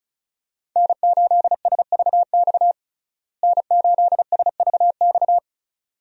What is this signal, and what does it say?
Morse code, heard as a keyed tone, about 34 wpm, 700 Hz. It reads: N8SVX N8SVX
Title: USA Call Signs - 5 Characters - Repeated 2x - 35wpm